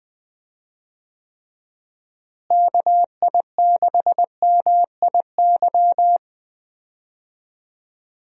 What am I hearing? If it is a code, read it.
Morse code, heard as a keyed tone, about 20 wpm, 700 Hz. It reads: KI6MIY
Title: USA Call Signs - 6 Characters - Repeated 2x - 20wpm